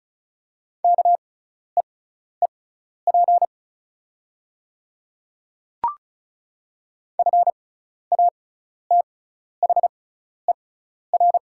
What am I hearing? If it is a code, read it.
Morse code, heard as a keyed tone, about 34 wpm, spaced as for 12 wpm, 700 Hz. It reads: KEEP FATHER
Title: 6x Character Spacing - Top 300 Words - 35wpm